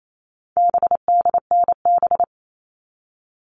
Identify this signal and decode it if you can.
Morse code, heard as a keyed tone, about 28 wpm, 700 Hz. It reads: BDNB
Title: All Cut Numbers - 4-Digit 28wpm